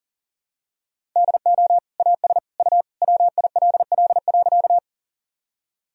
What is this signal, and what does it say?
Morse code, heard as a keyed tone, about 40 wpm, 700 Hz. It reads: DO AS U WILL.
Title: Sentences from Top 100 Words and Abbreviations 40wpm